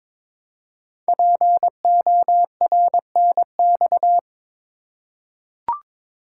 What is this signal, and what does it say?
Morse code, heard as a keyed tone, about 22 wpm, 700 Hz. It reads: PORNX
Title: Code Groups - 5 Letters - 22wpm